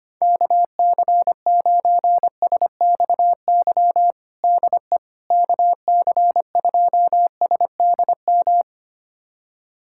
Morse code, heard as a keyed tone, about 25 wpm, 700 Hz. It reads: KC9SXY DE KC2SDM
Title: USA Callsign DE Callsign - 6 Characters - 25wpm